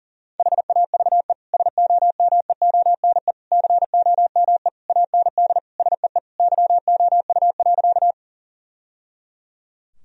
Morse code, heard as a keyed tone, about 40 wpm, 700 Hz. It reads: HAVE SOMEONE COME AND SEE YOU.
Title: Sentences from Top 1000 Words - Review Part 4 - 40wpm